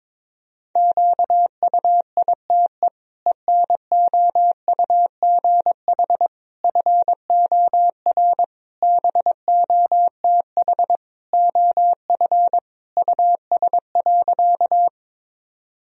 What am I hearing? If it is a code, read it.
Morse code, heard as a keyed tone, about 22 wpm, 700 Hz. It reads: QUITE ENOUGH FOR BOTH OF US.
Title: Sentences from Top 1000 Words - Review Part 1 - 22wpm